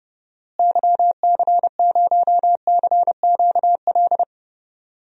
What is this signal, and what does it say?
Morse code, heard as a keyed tone, about 30 wpm, 700 Hz. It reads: YC0CQL
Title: DXCC-100 Call Signs - 6 Characters - Repeated 3x - 30wpm